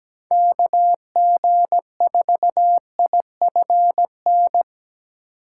Morse code, heard as a keyed tone, about 17 wpm, 700 Hz. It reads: K G 4 I F N
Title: USA Call Signs - 8x Character Spacing - 17wpm